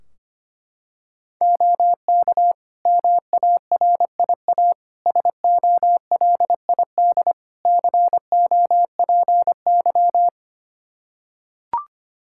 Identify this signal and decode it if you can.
Morse code, heard as a keyed tone, about 25 wpm, 700 Hz. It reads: OK MARIA SOLID COPY
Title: Practice QSO - Exchange 3 - More Info - 25wpm